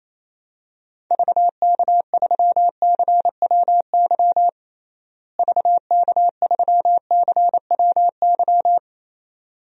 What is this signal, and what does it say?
Morse code, heard as a keyed tone, about 28 wpm, 700 Hz. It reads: VK3CWY VK3CWY
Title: DXCC-100 Call Signs - 6 Characters - Repeated 2x - 28wpm